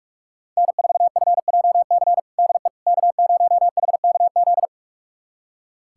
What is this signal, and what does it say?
Morse code, heard as a keyed tone, about 46 wpm, 700 Hz. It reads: N4FJC DE K0HKZ
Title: USA Callsign DE Callsign - 5 Characters - 45wpm